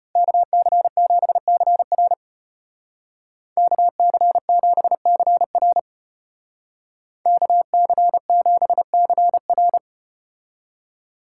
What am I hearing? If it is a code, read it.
Morse code, heard as a keyed tone, about 32 wpm, 700 Hz. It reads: KC7CR KC7CR KC7CR